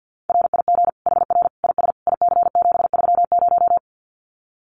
Morse code, heard as a keyed tone, about 50 wpm, 700 Hz. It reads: RIG HR IS IC730
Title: Common QSO Phrases 50wpm